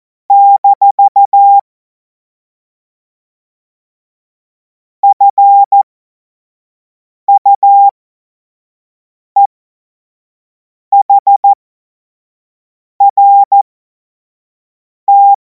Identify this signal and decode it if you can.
Morse code, heard as a keyed tone, about 14 wpm, spaced as for 5 wpm, 800 Hz. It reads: - FUEHRT
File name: Morsekurs / Witze / Witz 17. März 2026 25 F70